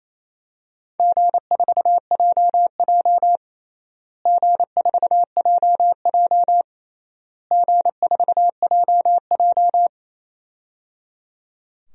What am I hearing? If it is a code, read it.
Morse code, heard as a keyed tone, about 28 wpm, 700 Hz. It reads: G4JJ G4JJ G4JJ
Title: DXCC-100 Call Signs - 4 Characters - Repeated 3x - 28wpm